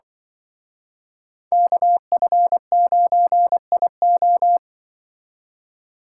Morse code, heard as a keyed tone, about 25 wpm, 700 Hz. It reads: KF9IO
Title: Speed Racing - USA 2x2 Call Signs 20wpm